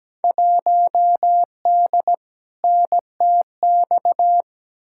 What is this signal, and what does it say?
Morse code, heard as a keyed tone, about 17 wpm, 700 Hz. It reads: 1D NTX
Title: Field Day Exchanges - 17wpm